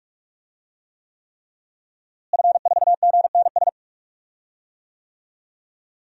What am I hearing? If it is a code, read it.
Morse code, heard as a keyed tone, about 46 wpm, 700 Hz. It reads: F4GNS
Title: DXCC-100 Call Signs - 5 Characters - Repeated 3x - 45wpm